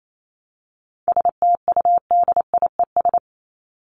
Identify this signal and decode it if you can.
Morse code, heard as a keyed tone, about 28 wpm, 700 Hz. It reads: STUDIES
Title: ES:S Suffix - Top 100 Words with Base Words 28wpm